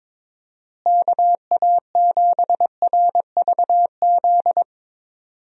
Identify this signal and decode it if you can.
Morse code, heard as a keyed tone, about 22 wpm, 700 Hz. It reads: KA7RVZ
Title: USA Call Signs - 6 Characters - Repeated 2x - 22wpm